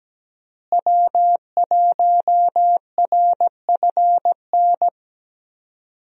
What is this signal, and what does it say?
Morse code, heard as a keyed tone, about 17 wpm, 700 Hz. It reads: W1RFN
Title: USA Call Signs - 5 Characters - Repeated 3x - 17wpm